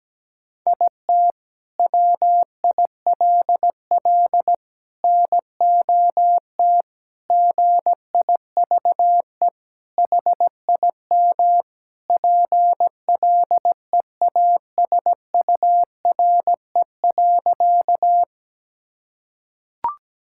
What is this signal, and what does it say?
Morse code, heard as a keyed tone, about 17 wpm, 700 Hz. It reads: IT WILL NOT GIVE HIM PLEASURE.
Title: Sentences from Top 1600 Words 17wpm